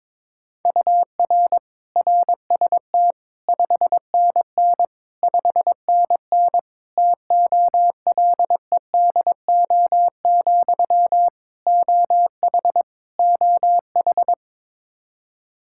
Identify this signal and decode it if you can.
Morse code, heard as a keyed tone, about 22 wpm, 700 Hz. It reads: UR RST 5NN 5NN TOLEDO, OH OH
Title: Practice QSO - Exchange 1 - Beginning - 22wpm